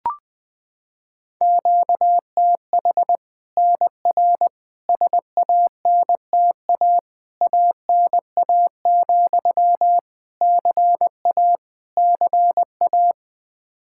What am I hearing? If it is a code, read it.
Morse code, heard as a keyed tone, about 21 wpm, 700 Hz. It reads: QTH NR SANTA ANA, CA CA
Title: Practice QSO - Exchange 1 - Beginning - 20wpm